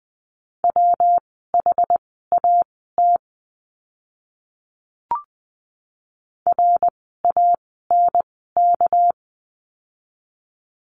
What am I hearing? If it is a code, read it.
Morse code, heard as a keyed tone, about 20 wpm, spaced as for 15 wpm, 700 Hz. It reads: WHAT RANK
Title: 199 Lesson 32f - Letter k - Words 20wpm